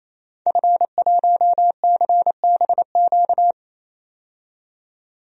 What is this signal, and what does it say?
Morse code, heard as a keyed tone, about 28 wpm, 700 Hz. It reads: F1CBQ
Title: DXCC-100 Call Signs - 5 Characters - 28wpm